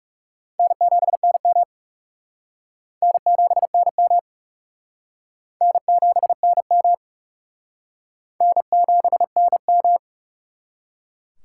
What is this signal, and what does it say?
Morse code, heard as a keyed tone, about 37 wpm, 700 Hz. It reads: N7NM N7NM N7NM N7NM
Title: Speed Racing - USA Call Signs 30wpm